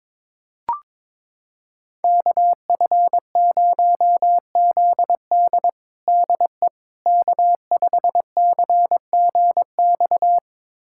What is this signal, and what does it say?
Morse code, heard as a keyed tone, about 22 wpm, 700 Hz. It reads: KF0ZD DE K5CGX
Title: USA Callsign DE Callsign - 5 Characters - 22wpm